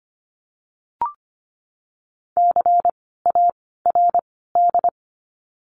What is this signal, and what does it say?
Morse code, heard as a keyed tone, about 25 wpm, 700 Hz. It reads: C A R D
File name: course / 25:10wpm / 094 Lesson 16f - Letter c - Words 25wpm